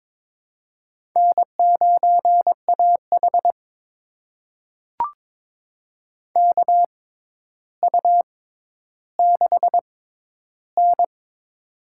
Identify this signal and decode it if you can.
Morse code, heard as a keyed tone, about 22 wpm, spaced as for 11 wpm, 700 Hz. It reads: N9AH KU6N
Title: USA Call Signs - 4 Characters - 6x Character Spacing - 22wpm